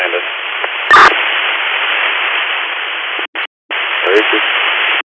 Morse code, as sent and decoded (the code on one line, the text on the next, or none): none